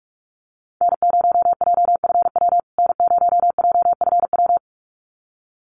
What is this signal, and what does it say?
Morse code, heard as a keyed tone, about 44 wpm, 700 Hz. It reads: N0JFW N0JFW
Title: USA Call Signs Repeated 2x 45wpm